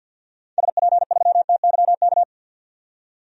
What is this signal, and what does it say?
Morse code, heard as a keyed tone, about 53 wpm, 700 Hz. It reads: SP3TYK
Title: DXCC-100 Call Signs - 6 Characters - 50wpm